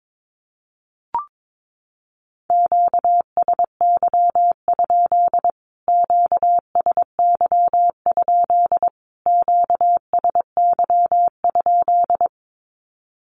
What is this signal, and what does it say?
Morse code, heard as a keyed tone, about 22 wpm, 700 Hz. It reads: QSY? QSY? QSY?
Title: Q-Codes Repeated 3x 22wpm